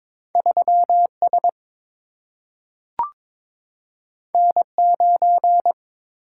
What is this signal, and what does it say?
Morse code, heard as a keyed tone, about 22 wpm, 700 Hz. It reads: 3 S N 9
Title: Single-Letter Single-Number 22wpm